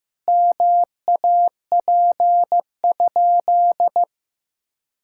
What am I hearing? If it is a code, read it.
Morse code, heard as a keyed tone, about 15 wpm, 700 Hz. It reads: MAP?
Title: Sentences from Top 1000 Words - Review Part 5 - 15wpm